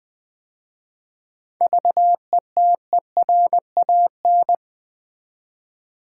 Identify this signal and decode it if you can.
Morse code, heard as a keyed tone, about 20 wpm, 700 Hz. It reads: VETERAN
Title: 7 Letter Words - First 3 Letters Given - 20wpm